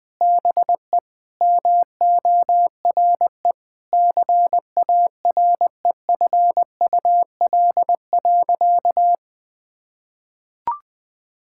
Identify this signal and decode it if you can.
Morse code, heard as a keyed tone, about 21 wpm, 700 Hz. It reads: BE MORE CAREFUL.
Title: Sentences from Top 1600 Words 20wpm